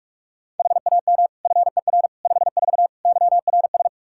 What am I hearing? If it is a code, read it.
Morse code, heard as a keyed tone, about 46 wpm, 700 Hz. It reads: HAM FER 54 YRS
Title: Practice QSO - Exchange 3 - More Info - 45wpm